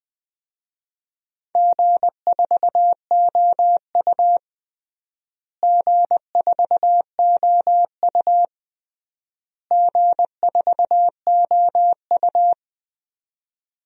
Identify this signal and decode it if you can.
Morse code, heard as a keyed tone, about 21 wpm, 700 Hz. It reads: G4OU G4OU G4OU
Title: DXCC-100 Call Signs - 4 Characters - Repeated 3x - 20wpm